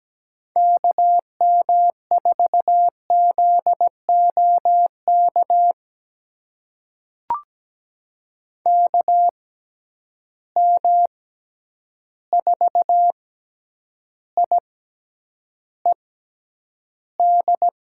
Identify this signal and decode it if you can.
Morse code, heard as a keyed tone, about 17 wpm, spaced as for 9 wpm, 700 Hz. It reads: KM4ZOK KM4IED
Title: USA Call Signs - 6 Characters - 6x Character Spacing - 17wpm